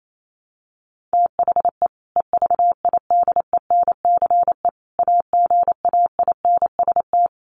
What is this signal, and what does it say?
Morse code, heard as a keyed tone, about 28 wpm, 700 Hz. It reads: THE EVIDENCE AGAINST
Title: Sentences from Top 1300 Words 28wpm